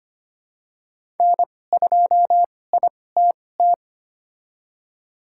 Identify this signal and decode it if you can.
Morse code, heard as a keyed tone, about 25 wpm, spaced as for 18 wpm, 700 Hz. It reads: N2ITT
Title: USA Call Signs - 5 Characters - 2x Character Spacing - 25wpm